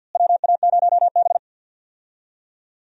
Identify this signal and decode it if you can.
Morse code, heard as a keyed tone, about 48 wpm, 700 Hz. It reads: WA0B
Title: USA Call Signs - 4 Characters - Repeated 3x - 50wpm